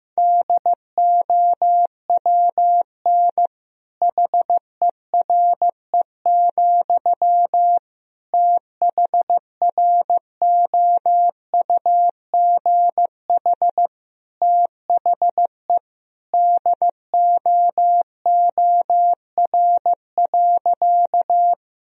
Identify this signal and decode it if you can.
Morse code, heard as a keyed tone, about 15 wpm, 700 Hz. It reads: DOWN HERE, THROUGH THE DOOR.
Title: Sentences from Top 1000 Words - Review Part 1 - 15wpm